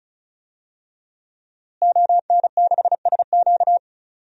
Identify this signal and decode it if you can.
Morse code, heard as a keyed tone, about 36 wpm, 700 Hz. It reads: ON6SQ
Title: DXCC-100 Call Signs - 5 Characters - Repeated 3x - 35wpm